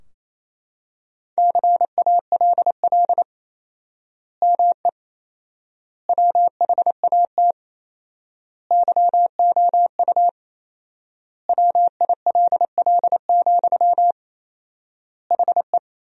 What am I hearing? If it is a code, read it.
Morse code, heard as a keyed tone, about 28 wpm, 700 Hz. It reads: CALL ME WHAT YOU WILL, HE
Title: 4x Word-Spacing - Sentences from Top 200 Words - 28wpm